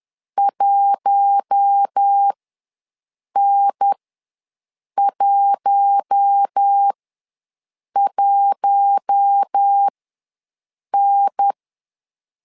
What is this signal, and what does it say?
Morse code, heard as a keyed tone, about 10 wpm, spaced as for 6 wpm, 800 Hz. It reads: 1N11N
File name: LEZIONI / 4 Q1 ws 10wpm 2538